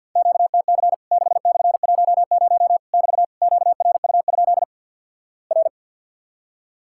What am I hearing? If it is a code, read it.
Morse code, heard as a keyed tone, about 46 wpm, 700 Hz. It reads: QTC 6/10 <BT> QRU? R
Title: Worked All Europe DX Contest - QTC Exchange with Repeats - 50wpm